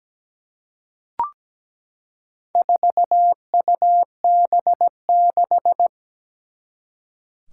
Morse code, heard as a keyed tone, about 17 wpm, 700 Hz. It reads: 4UB6
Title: Four Challenging Letter-Number Combinations 17wpm